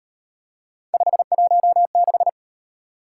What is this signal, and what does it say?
Morse code, heard as a keyed tone, about 38 wpm, 700 Hz. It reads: 516